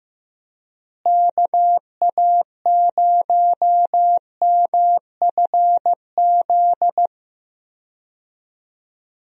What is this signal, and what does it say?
Morse code, heard as a keyed tone, about 15 wpm, 700 Hz. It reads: KA0MFZ
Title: USA Call Signs - 6 Characters - Repeated 3x - 15wpm